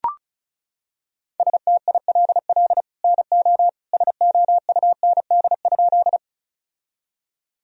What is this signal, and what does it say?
Morse code, heard as a keyed tone, about 36 wpm, 700 Hz. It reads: STILL NO SOUND?
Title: Sentences from Top 1000 Words - Review Part 1 - 35wpm